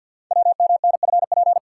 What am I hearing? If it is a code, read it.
Morse code, heard as a keyed tone, about 48 wpm, 700 Hz. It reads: WMNFP
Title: Code Groups - 5 Letters - 50wpm